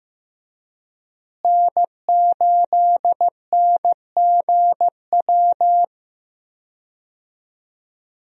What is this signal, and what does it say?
Morse code, heard as a keyed tone, about 15 wpm, 700 Hz. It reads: N8NGW
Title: USA Call Signs - 5 Characters - Repeated 2x - 15wpm